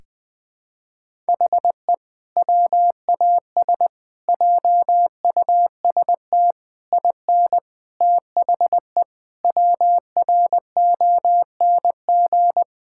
Morse code, heard as a keyed tone, about 21 wpm, 700 Hz. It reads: HE WAS JUST IN THE WRONG
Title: Sentences with 7-Word Phrases - 20wpm